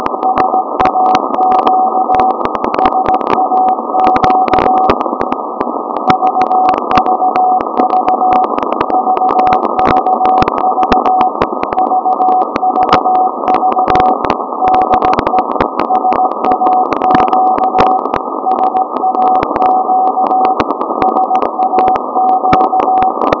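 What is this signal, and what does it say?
Morse code, heard as a keyed tone, about 17 wpm, 750 Hz, with noise. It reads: SAQ SAQ VVV VVV VVV DE SAQ SAQ SAQ
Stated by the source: SAQ VLF transmission, audio signal at 0.750 kHz, RF signal at 17.2 kHz.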